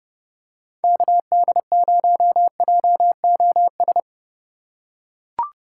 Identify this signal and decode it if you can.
Morse code, heard as a keyed tone, about 30 wpm, 700 Hz. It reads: KD0JOS
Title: USA Call Signs - 4x Character Spacing - 30wpm